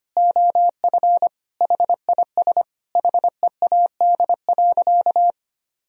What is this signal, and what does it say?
Morse code, heard as a keyed tone, about 25 wpm, 700 Hz. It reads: OF HIS HEAD.
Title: Sentences from Top 1000 Words - Review Part 3 - 25wpm